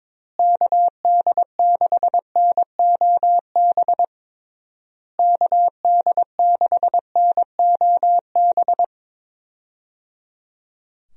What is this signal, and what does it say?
Morse code, heard as a keyed tone, about 22 wpm, 700 Hz. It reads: KD6NOB KD6NOB
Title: USA Call Signs - 6 Characters - Repeated 3x - 22wpm